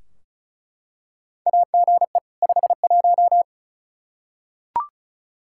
Morse code, heard as a keyed tone, about 36 wpm, 700 Hz. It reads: AGE 51
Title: Practice QSO - Exchange 3 - More Info - 35wpm